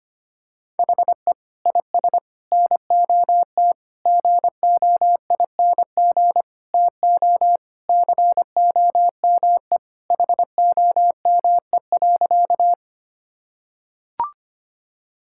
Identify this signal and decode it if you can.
Morse code, heard as a keyed tone, about 25 wpm, 700 Hz. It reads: HE IS NOT GOING TO COME HOME.
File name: practice / 25wpm / Sentences with 4-Word Phrases - 25wpm